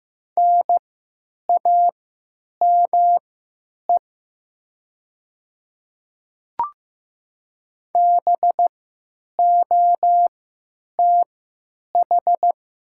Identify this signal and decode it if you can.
Morse code, heard as a keyed tone, about 15 wpm, spaced as for 9 wpm, 700 Hz. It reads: NAME BOTH
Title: 3x Character Spacing - Top 300 Words - 15wpm